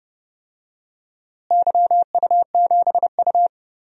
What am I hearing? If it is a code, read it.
Morse code, heard as a keyed tone, about 32 wpm, 700 Hz. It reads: YU7U
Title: DXCC-100 Call Signs - 4 Characters - Repeated 3x - 30wpm